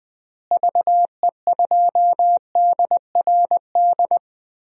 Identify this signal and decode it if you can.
Morse code, heard as a keyed tone, about 20 wpm, 700 Hz. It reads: VE2DRD
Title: DXCC-100 Call Signs - 6 Characters - Repeated 3x - 20wpm